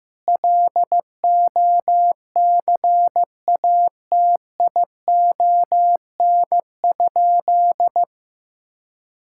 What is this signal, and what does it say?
Morse code, heard as a keyed tone, about 15 wpm, 700 Hz. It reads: LOCATION?
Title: Sentences from Top 1000 Words - Review Part 6 - 15wpm